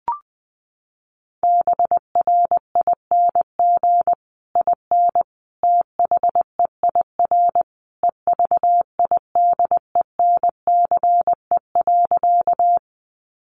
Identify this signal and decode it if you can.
Morse code, heard as a keyed tone, about 19 wpm, 700 Hz. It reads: BRING IN THEIR EVIDENCE.
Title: Sentences from Top 1300 Words 20wpm